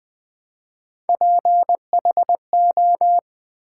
Morse code, heard as a keyed tone, about 20 wpm, 700 Hz. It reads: PHO
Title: Build A Word - 1 Letter At A Time - 5-Letter Words - 20wpm